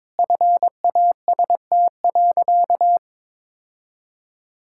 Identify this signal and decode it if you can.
Morse code, heard as a keyed tone, about 22 wpm, 700 Hz. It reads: FAST.
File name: practice / 22wpm / Sentences with 2-Word Phrases - 22wpm